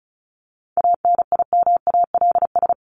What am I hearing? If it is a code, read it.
Morse code, heard as a keyed tone, about 34 wpm, 700 Hz. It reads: ANIMALS
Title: ES:S Suffix - Top 100 Words with Base Words 35wpm